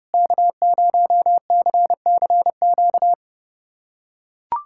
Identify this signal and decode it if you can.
Morse code, heard as a keyed tone, about 30 wpm, 700 Hz. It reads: K0CCQ
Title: USA Callsign DE Callsign - 5 Characters - 30wpm